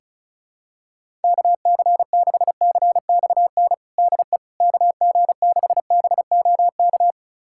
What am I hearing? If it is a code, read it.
Morse code, heard as a keyed tone, about 34 wpm, 700 Hz. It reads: KC6CXN DE KG6BOK
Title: USA Callsign DE Callsign - 6 Characters - 35wpm